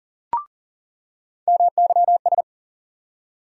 Morse code, heard as a keyed tone, about 40 wpm, 700 Hz. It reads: MYS
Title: USA Call Sign Postfixes - 3 Letters 40wpm